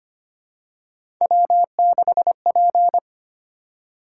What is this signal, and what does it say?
Morse code, heard as a keyed tone, about 25 wpm, 700 Hz. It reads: W6P
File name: Three Letter-Number Combinations - 25wpm